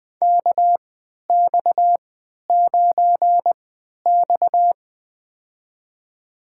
Morse code, heard as a keyed tone, about 21 wpm, spaced as for 11 wpm, 700 Hz. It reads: KX9X
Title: USA Call Signs - 4 Characters - 3x Character Spacing - 20wpm